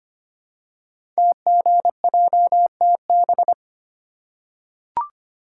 Morse code, heard as a keyed tone, about 25 wpm, 700 Hz. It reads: TGJTB
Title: Code Groups - 5 Letters - 25wpm